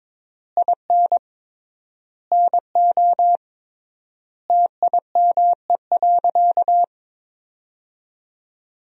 Morse code, heard as a keyed tone, about 22 wpm, 700 Hz. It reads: IN NO TIME.
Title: 3x Word-Spacing - Sentences from Top 400 Words - 22wpm